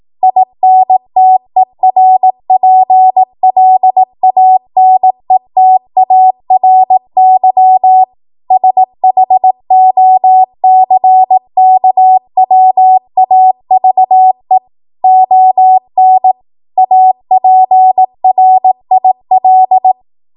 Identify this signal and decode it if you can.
Morse code, heard as a keyed tone, about 18 wpm, 750 Hz. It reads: INTERPLANETARY SHOCKWAVE ON APRIL
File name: Morse / 18WPM